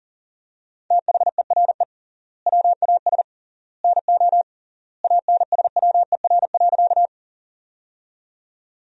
Morse code, heard as a keyed tone, about 40 wpm, 700 Hz. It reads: THERE WAS NO ANSWER.
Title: 3x Word-Spacing - Sentences from Top 300 Words - 40wpm